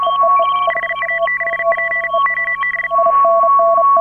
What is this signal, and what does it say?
Morse code, heard as a keyed tone, about 38 wpm, 1950 Hz. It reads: HA6P OB
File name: AUDIO2018 / HA9A CW